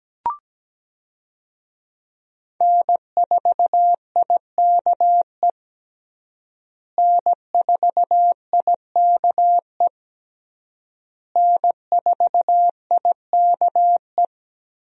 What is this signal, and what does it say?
Morse code, heard as a keyed tone, about 17 wpm, 700 Hz. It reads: N4IKE N4IKE N4IKE
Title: USA Call Signs - 5 Characters - Repeated 3x - 17wpm